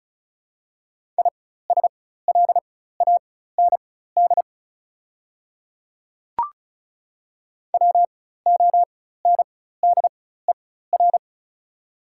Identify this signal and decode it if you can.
Morse code, heard as a keyed tone, about 34 wpm, spaced as for 16 wpm, 700 Hz. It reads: ISLAND WONDER
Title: 4x Character Spacing - Top 500 Words - 35wpm